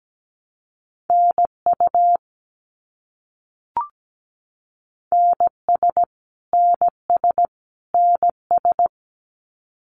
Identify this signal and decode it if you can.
Morse code, heard as a keyed tone, about 17 wpm, 700 Hz. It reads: NU NS NS NS
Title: Canadian Provinces and Territories - Repeated 3x - 17wpm